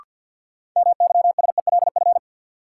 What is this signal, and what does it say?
Morse code, heard as a keyed tone, about 48 wpm, 700 Hz. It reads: MYSELF
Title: Sentences from Top 1300 Words - New Word Drill Alt - 50wpm